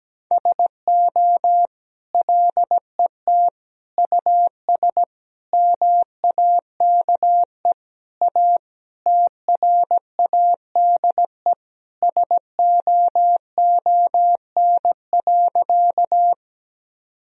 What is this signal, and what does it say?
Morse code, heard as a keyed tone, about 17 wpm, 700 Hz. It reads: SO LET US MAKE A TRADE SOON.
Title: Sentences from Top 1000 Words - Review Part 4 - 17wpm